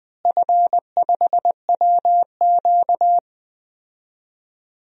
Morse code, heard as a keyed tone, about 20 wpm, 700 Hz. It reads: F5WQ
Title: DXCC-100 Call Signs - 4 Characters - Repeated 3x - 20wpm